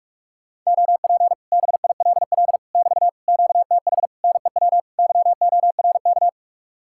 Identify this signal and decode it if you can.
Morse code, heard as a keyed tone, about 49 wpm, 700 Hz. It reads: OP BILL <BT> QTH NEW YORK